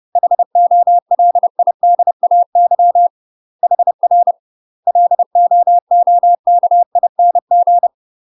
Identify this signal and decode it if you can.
Morse code, heard as a keyed tone, about 30 wpm, 700 Hz. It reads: HOLIDAY HR LOOKING
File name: UK MASTER QSOs TRAINING / 30WPM / UK MASTER QSOs 0008